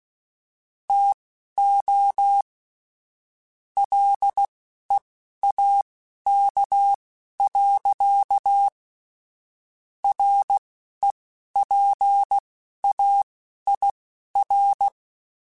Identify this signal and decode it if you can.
Morse code, heard as a keyed tone, about 16 wpm, spaced as for 11 wpm, 800 Hz. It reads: TO LEAK. REPAIR
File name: Chuckles / Chuckles-mp3s / Guaranteed @10